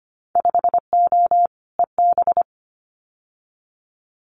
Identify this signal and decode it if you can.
Morse code, heard as a keyed tone, about 25 wpm, 700 Hz. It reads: 5O EB
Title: Winter Field Day Exchanges Encore 25wpm